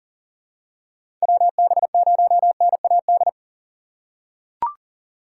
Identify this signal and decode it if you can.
Morse code, heard as a keyed tone, about 40 wpm, 700 Hz. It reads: WB0NAD
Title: USA Call Signs - 6 Characters - 6x Character Spacing - 40wpm